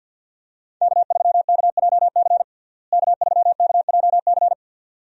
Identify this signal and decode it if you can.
Morse code, heard as a keyed tone, about 50 wpm, 700 Hz. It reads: K3KJC K3KJC
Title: USA Call Signs - 5 Characters - Repeated 2x - 50wpm